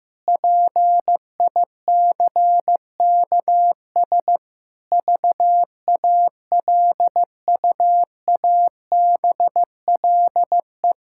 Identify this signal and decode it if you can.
Morse code, heard as a keyed tone, about 15 wpm, 700 Hz. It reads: PICKS VALUABLE
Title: Pangram Challenge v2 - 15wpm